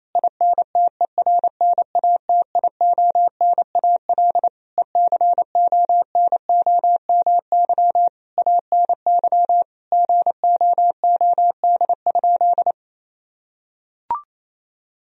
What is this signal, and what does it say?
Morse code, heard as a keyed tone, about 27 wpm, 700 Hz. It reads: INTERNATIONAL ECONOMY ANY GOOD?
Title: Sentences from Top 1000 Words - Review Part 4 - 28wpm